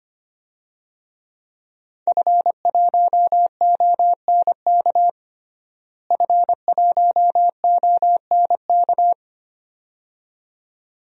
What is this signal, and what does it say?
Morse code, heard as a keyed tone, about 25 wpm, 700 Hz. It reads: F1ONK F1ONK
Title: DXCC-100 Call Signs - 5 Characters - Repeated 2x - 25wpm